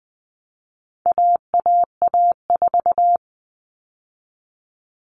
Morse code, heard as a keyed tone, about 20 wpm, 700 Hz. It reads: AAA4
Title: All Cut Numbers - 4-Digit 20wpm